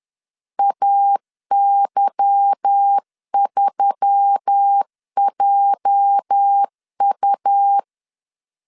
Morse code, heard as a keyed tone, about 10 wpm, 800 Hz. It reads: AY3JU
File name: LEZIONI / 8 YJ45 / 8 YJ 10wpm 2651